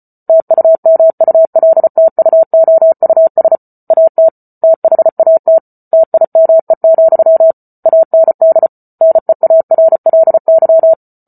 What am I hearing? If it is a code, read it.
Morse code, heard as a keyed tone, about 34 wpm, 650 Hz. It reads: TUMULTUOUS AT THAT TIME, AND NEARLY